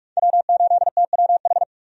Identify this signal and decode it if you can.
Morse code, heard as a keyed tone, about 43 wpm, 700 Hz. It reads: W8TWH
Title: USA Call Signs - 5 Characters - 45wpm